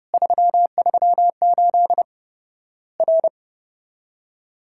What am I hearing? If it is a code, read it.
Morse code, heard as a keyed tone, about 31 wpm, 700 Hz. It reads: 338 R
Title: Worked All Europe DX Contest - QTC Exchange - 30wpm